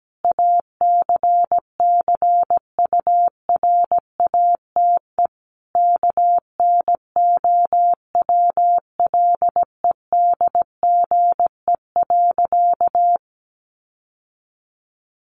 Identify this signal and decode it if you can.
Morse code, heard as a keyed tone, about 17 wpm, 700 Hz. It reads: ACCURATE KNOWLEDGE.
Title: Sentences from Top 1200 Words 17wpm